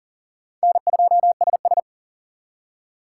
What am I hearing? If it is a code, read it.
Morse code, heard as a keyed tone, about 49 wpm, 700 Hz. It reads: N2SS
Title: Speed Racing - USA 1x2 Call Signs 40wpm